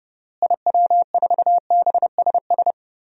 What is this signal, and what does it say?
Morse code, heard as a keyed tone, about 31 wpm, 700 Hz. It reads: IW4BSS
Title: DXCC-100 Call Signs - 6 Characters - Repeated 3x - 30wpm